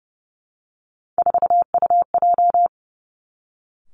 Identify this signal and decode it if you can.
Morse code, heard as a keyed tone, about 30 wpm, 700 Hz. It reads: 4UJ
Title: Three Challenging Letter-Number Combinations 30wpm